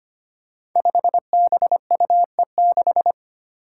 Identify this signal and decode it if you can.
Morse code, heard as a keyed tone, about 25 wpm, 700 Hz. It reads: 5BUE6
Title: Five Challenging Letter-Number Combinations 25wpm